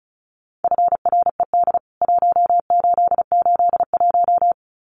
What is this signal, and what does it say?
Morse code, heard as a keyed tone, about 34 wpm, 700 Hz. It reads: FRED 1881
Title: CWT Exchange 35wpm